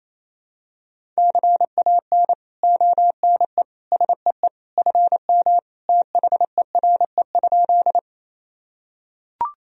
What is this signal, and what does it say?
Morse code, heard as a keyed tone, about 27 wpm, 700 Hz. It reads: CAN ONE SEE FM THERE?
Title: Sentences from Top 100 Words and Abbreviations 28wpm